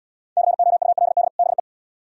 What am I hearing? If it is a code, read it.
Morse code, heard as a keyed tone, about 76 wpm, 700 Hz. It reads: COULD BE